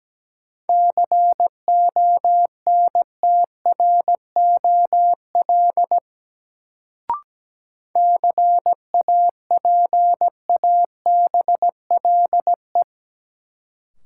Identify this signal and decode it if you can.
Morse code, heard as a keyed tone, about 17 wpm, 700 Hz. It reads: CONTROL CAPABLE
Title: Sentences from Top 1200 Words - New Word Drill Alt - 17wpm